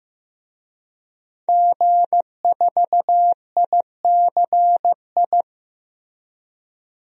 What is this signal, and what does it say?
Morse code, heard as a keyed tone, about 15 wpm, 700 Hz. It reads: G4ICI
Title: DXCC-100 Call Signs - 5 Characters - 15wpm